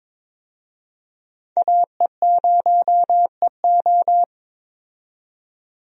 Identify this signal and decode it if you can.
Morse code, heard as a keyed tone, about 22 wpm, 700 Hz. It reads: AE0EO
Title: USA Call Signs - 5 Characters - 22wpm